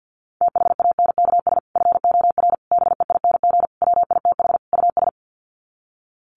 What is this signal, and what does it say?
Morse code, heard as a keyed tone, about 50 wpm, 700 Hz. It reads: THANKS FOR BEING WITH US
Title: Sets of 5 Words 50wpm